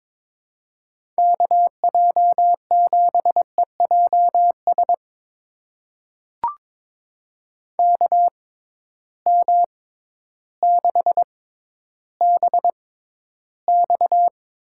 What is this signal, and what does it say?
Morse code, heard as a keyed tone, about 21 wpm, spaced as for 12 wpm, 700 Hz. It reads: KJ7EJS KM6BX
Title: USA Call Signs - 6 Characters - 6x Character Spacing - 22wpm